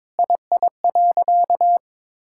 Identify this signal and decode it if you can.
Morse code, heard as a keyed tone, about 22 wpm, 700 Hz. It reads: II.
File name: Sentences with 7-Word Phrases - 22wpm